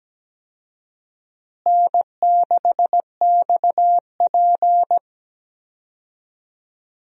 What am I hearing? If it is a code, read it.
Morse code, heard as a keyed tone, about 17 wpm, 700 Hz. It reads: N6XP
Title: USA Call Signs - 4 Characters - Repeated 2x - 17wpm